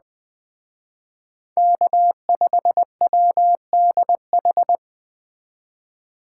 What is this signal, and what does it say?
Morse code, heard as a keyed tone, about 20 wpm, 700 Hz. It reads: K5WDH